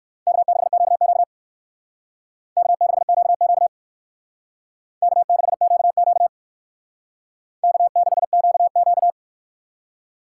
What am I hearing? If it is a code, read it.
Morse code, heard as a keyed tone, about 57 wpm, 700 Hz. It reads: K6QQ K6QQ K6QQ K6QQ